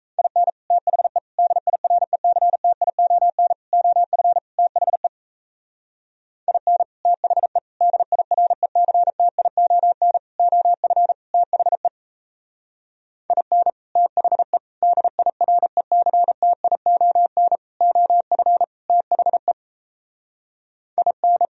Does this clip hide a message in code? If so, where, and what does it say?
Yes: Morse code from 0.2 to 21.4 s, heard as a keyed tone, about 35 wpm, 700 Hz. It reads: IN THE DIRECTION OF THE IN THE DIRECTION OF THE IN THE DIRECTION OF THE IN